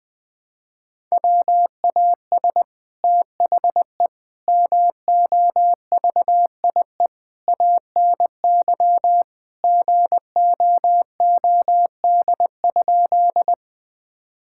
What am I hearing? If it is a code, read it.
Morse code, heard as a keyed tone, about 20 wpm, 700 Hz. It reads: WAS THE MOVIE ANY GOOD?
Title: Sentences from Top 1000 Words - Review Part 5 - 20wpm